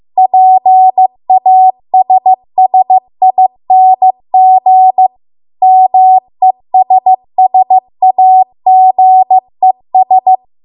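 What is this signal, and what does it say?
Morse code, heard as a keyed tone, about 15 wpm, 750 Hz. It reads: PASSING MESSAGES
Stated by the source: Morse code practice at 15 wpm from the national amateur radio association's code archive.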